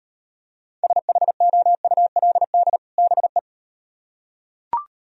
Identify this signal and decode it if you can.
Morse code, heard as a keyed tone, about 38 wpm, 700 Hz. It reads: SHOULD BE